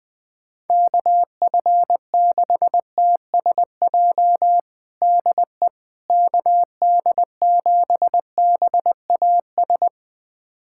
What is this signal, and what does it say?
Morse code, heard as a keyed tone, about 20 wpm, 700 Hz. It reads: KF6TSJ DE KD7BAS
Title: USA Callsign DE Callsign - 6 Characters - 20wpm